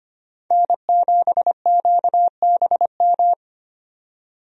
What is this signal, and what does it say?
Morse code, heard as a keyed tone, about 25 wpm, 700 Hz. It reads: N7QBM
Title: USA Call Signs - 5 Characters - 25wpm